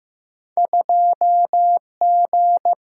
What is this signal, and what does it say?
Morse code, heard as a keyed tone, about 15 wpm, 700 Hz. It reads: 2G
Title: Single-Letter Single-Number 15wpm